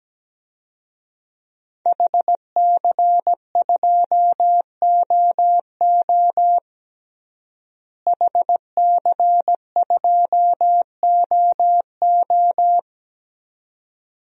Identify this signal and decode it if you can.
Morse code, heard as a keyed tone, about 17 wpm, 700 Hz. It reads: HC2OO HC2OO
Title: DXCC-100 Call Signs - 5 Characters - Repeated 2x - 17wpm